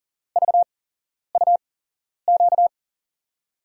Morse code, heard as a keyed tone, about 40 wpm, spaced as for 11 wpm, 700 Hz. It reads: VUQ
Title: USA Call Signs - 8x Character Spacing - 40wpm